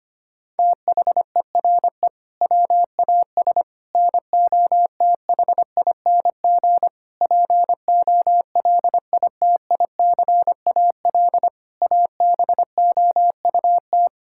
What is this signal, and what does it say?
Morse code, heard as a keyed tone, about 25 wpm, 700 Hz. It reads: THERE WAS NOTHING POLITICAL ABOUT
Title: Sentences from Top 1000 Words - Review Part 5 - 25wpm